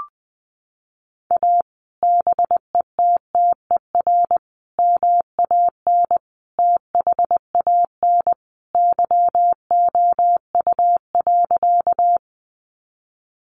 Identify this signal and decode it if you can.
Morse code, heard as a keyed tone, about 20 wpm, 700 Hz. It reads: A BETTER MAN THAN YOU.
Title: Sentences from Top 400 Words 20wpm